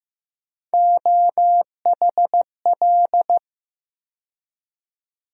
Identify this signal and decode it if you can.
Morse code, heard as a keyed tone, about 15 wpm, 700 Hz. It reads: OHL
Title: USA Call Sign Postfixes - 3 Letters 15wpm